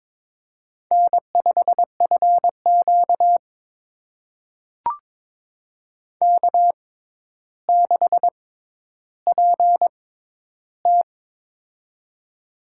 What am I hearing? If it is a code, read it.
Morse code, heard as a keyed tone, about 23 wpm, spaced as for 11 wpm, 700 Hz. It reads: N5FQ K6PT
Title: USA Call Signs - 4 Characters - 6x Character Spacing - 22wpm